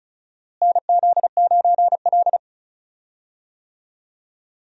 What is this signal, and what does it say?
Morse code, heard as a keyed tone, about 36 wpm, 700 Hz. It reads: NZ9L
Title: USA Call Signs - 4 Characters - Repeated 3x - 35wpm